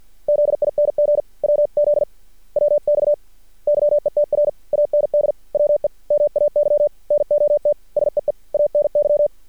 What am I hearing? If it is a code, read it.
Morse code, heard as a keyed tone, about 53 wpm, 590 Hz. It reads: ZING WZ WX YETR AND WE MAY NOT SEE ANY
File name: archived / CRSnet